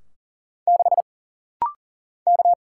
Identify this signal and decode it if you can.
Morse code, heard as a keyed tone, about 40 wpm, 700 Hz. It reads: 6 K
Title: Single Letter-Number - Rapid-Fire 40wpm